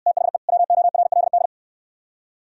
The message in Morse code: THE WORLD

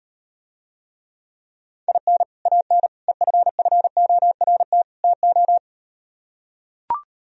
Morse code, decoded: IN AN EFFORT TO

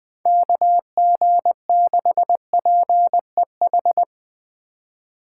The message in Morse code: KG6PEH

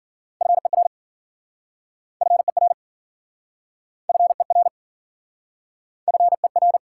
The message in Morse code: FER FER FER FER

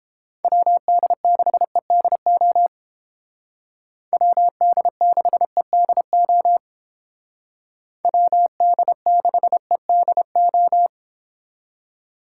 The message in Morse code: WD6EDO WD6EDO WD6EDO

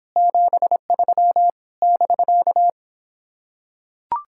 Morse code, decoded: 73 <BK>

